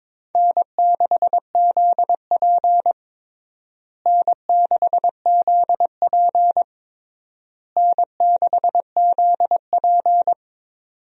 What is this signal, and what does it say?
Morse code, heard as a keyed tone, about 22 wpm, 700 Hz. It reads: N6ZP N6ZP N6ZP
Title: USA Call Signs - 4 Characters - Repeated 3x - 22wpm